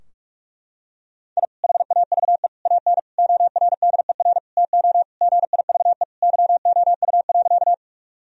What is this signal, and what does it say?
Morse code, heard as a keyed tone, about 46 wpm, 700 Hz. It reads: I HAVE AN ORDER TO GIVE YOU.
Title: Sentences from Top 1000 Words - Review Part 2 - 45wpm